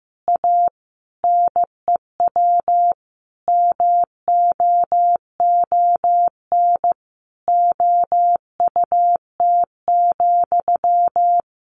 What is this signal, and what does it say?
Morse code, heard as a keyed tone, about 15 wpm, 700 Hz. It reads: A NEW MOON OUT,
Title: Sentences from Top 500 Words 15wpm